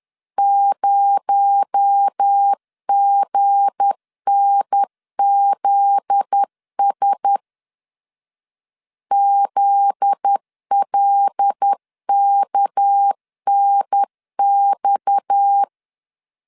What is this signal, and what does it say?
Morse code, heard as a keyed tone, about 10 wpm, 800 Hz. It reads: 0GNZS ZLKNX
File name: LEZIONI / 7 GF9Z / 7 GF9Z 10wpm 2611